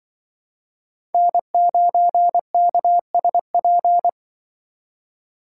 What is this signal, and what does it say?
Morse code, heard as a keyed tone, about 25 wpm, 700 Hz. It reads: N9KSP
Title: Speed Racing - USA 1x3 Call Signs 20wpm